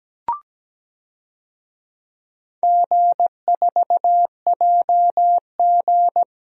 Morse code, heard as a keyed tone, about 17 wpm, 700 Hz. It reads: G4JG
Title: DXCC-100 Call Signs - 4 Characters - 17wpm